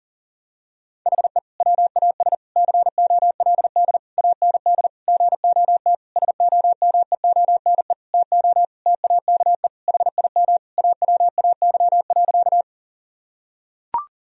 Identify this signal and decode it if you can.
Morse code, heard as a keyed tone, about 40 wpm, 700 Hz. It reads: HE WAS COLD AND GOT SOMEONE TO TAKE HIM AWAY.
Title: Sentences from Top 1000 Words - Review Part 4 - 40wpm